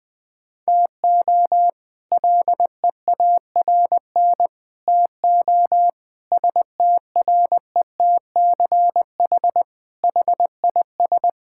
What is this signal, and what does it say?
Morse code, heard as a keyed tone, about 21 wpm, 700 Hz. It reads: TO LEARN TO STRETCH HIS
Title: All-Age Appropriate Jokes - 20wpm